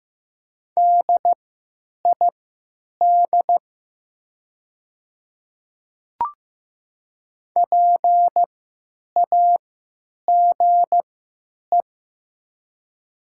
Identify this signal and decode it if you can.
Morse code, heard as a keyed tone, about 15 wpm, spaced as for 9 wpm, 700 Hz. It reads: DID PAGE
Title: 3x Character Spacing - Top 500 Words - 15wpm